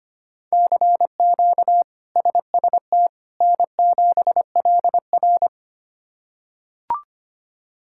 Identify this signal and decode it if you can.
Morse code, heard as a keyed tone, about 25 wpm, 700 Hz. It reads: CQ SST N7LR
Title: K1USN SST QSO Practice - 25wpm